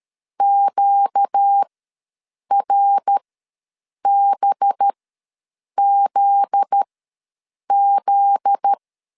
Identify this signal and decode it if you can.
Morse code, heard as a keyed tone, about 13 wpm, spaced as for 7 wpm, 800 Hz. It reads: QRBZZ
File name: LEZIONI / 7 GF9Z / 7 GF9Z ws 12wpm 3321